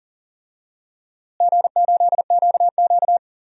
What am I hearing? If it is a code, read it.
Morse code, heard as a keyed tone, about 40 wpm, 700 Hz. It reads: G8QQ
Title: DXCC-100 Call Signs - 4 Characters - Repeated 2x - 40wpm